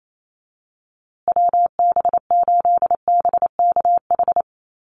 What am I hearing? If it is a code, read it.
Morse code, heard as a keyed tone, about 28 wpm, 700 Hz. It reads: WB8BKH
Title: USA Call Signs Repeated 3x 28wpm